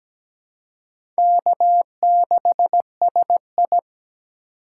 Morse code, heard as a keyed tone, about 17 wpm, 700 Hz. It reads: K6SI
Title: USA Call Signs - 4 Characters - 4x Character Spacing - 17wpm